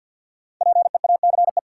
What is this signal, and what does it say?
Morse code, heard as a keyed tone, about 52 wpm, 700 Hz. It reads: PEACE
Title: Sentences from Top 1500 Words - New Word Drill Alt - 50wpm